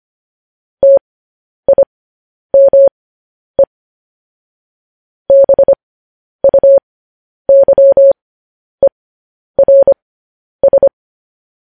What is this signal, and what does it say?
Morse code, heard as a keyed tone, about 25 wpm, spaced as for 10 wpm, 550 Hz. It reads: TIME BUYERS
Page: News Headlines in Morse Code at 10 WPM